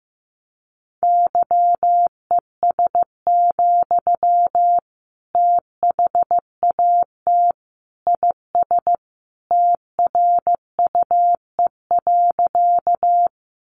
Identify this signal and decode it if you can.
Morse code, heard as a keyed tone, about 15 wpm, 700 Hz. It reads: YES, THAT IS TRUE.
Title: Sentences from Top 500 Words 15wpm